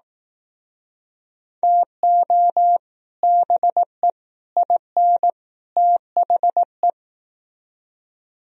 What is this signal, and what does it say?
Morse code, heard as a keyed tone, about 18 wpm, 700 Hz. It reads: TO BE IN THE
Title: Speed-Racing - Sets of 4 Words 15wpm